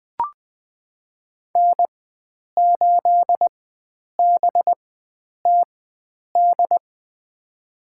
Morse code, heard as a keyed tone, about 20 wpm, spaced as for 9 wpm, 700 Hz. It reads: N8BTD
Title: USA Call Signs - 5 Characters - 4x Character Spacing - 20wpm